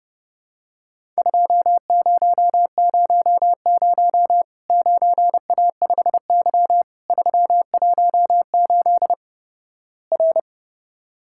Morse code, heard as a keyed tone, about 31 wpm, 700 Hz. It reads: 2000 9A5Y 318 R
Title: Worked All Europe DX Contest - QTC Exchange - 30wpm